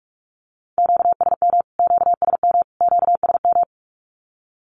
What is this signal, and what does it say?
Morse code, heard as a keyed tone, about 44 wpm, 700 Hz. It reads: QSM QSM QSM
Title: Q-Codes Repeated 3x 45wpm